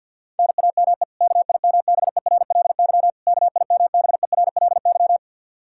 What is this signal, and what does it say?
Morse code, heard as a keyed tone, about 50 wpm, 700 Hz. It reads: NAME KIMBERLY KIMBERLY
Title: Practice QSO - Exchange 1 - Beginning - 50wpm